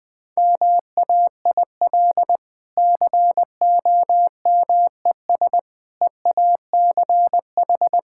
Morse code, heard as a keyed tone, about 19 wpm, 700 Hz. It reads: MAIL COMES EACH
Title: Easy Sentences with Hints - 20wpm